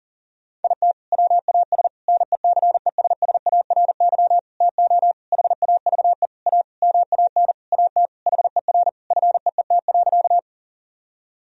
Morse code, heard as a keyed tone, about 40 wpm, 700 Hz. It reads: IT WAS NECESSARY TO HAVE A MAN AT HER FEET.
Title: Sentences from Top 1000 Words - Review Part 1 - 40wpm